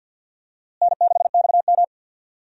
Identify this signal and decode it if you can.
Morse code, heard as a keyed tone, about 53 wpm, 700 Hz. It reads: N6XM
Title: USA Call Signs - 4 Characters - Repeated 2x - 50wpm